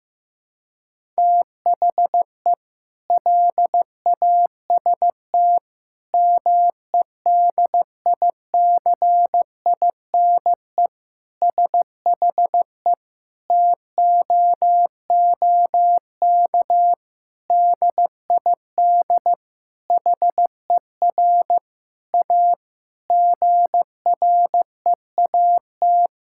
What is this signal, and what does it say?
Morse code, heard as a keyed tone, about 15 wpm, 700 Hz. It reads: THE LAST MEDICINE SHE TOOK DID HER A GREAT
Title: Sentences from Top 1000 Words - Review Part 6 - 15wpm